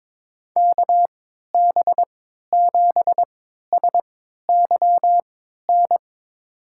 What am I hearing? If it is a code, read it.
Morse code, heard as a keyed tone, about 22 wpm, spaced as for 13 wpm, 700 Hz. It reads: KB7SYN
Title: USA Call Signs - 6 Characters - 3x Character Spacing - 22wpm